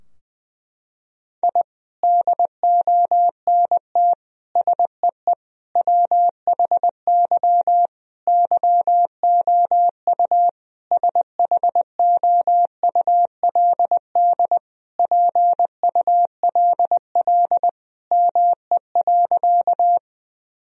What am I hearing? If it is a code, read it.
Morse code, heard as a keyed tone, about 20 wpm, 700 Hz. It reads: I DONT SEE WHY YOU SHOULD PULL ME.
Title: Sentences from Top 1000 Words - Review Part 2 - 20wpm